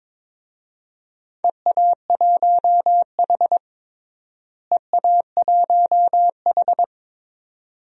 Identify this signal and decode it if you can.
Morse code, heard as a keyed tone, about 21 wpm, 700 Hz. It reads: EA1H EA1H
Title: DXCC-100 Call Signs - 4 Characters - Repeated 3x - 22wpm